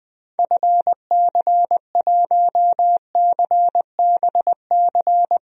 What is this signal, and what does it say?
Morse code, heard as a keyed tone, about 20 wpm, 700 Hz. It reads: FC1CBC
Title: DXCC-100 Call Signs - 6 Characters - 20wpm